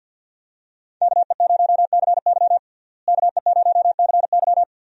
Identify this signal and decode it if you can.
Morse code, heard as a keyed tone, about 49 wpm, 700 Hz. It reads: KE0CY KE0CY
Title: USA Call Signs - 5 Characters - Repeated 2x - 50wpm